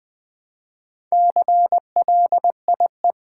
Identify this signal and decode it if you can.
Morse code, heard as a keyed tone, about 20 wpm, 700 Hz. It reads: CLIE
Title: Build A Word - 2 Letters At A Time - 6-Letter Words - 20wpm